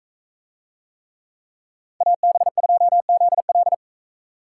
Morse code, heard as a keyed tone, about 44 wpm, 700 Hz. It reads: AB2ZL